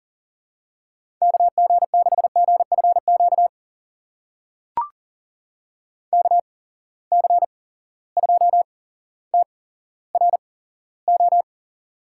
Morse code, heard as a keyed tone, about 38 wpm, spaced as for 17 wpm, 700 Hz. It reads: KG6GFQ KC2TRO
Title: USA Call Signs - 6 Characters - 8x Character Spacing - 40wpm